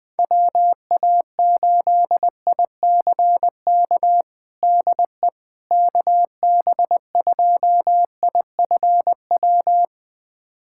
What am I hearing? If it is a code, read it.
Morse code, heard as a keyed tone, about 20 wpm, 700 Hz. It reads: WA8ICK DE KB2IFW
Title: USA Callsign DE Callsign 20wpm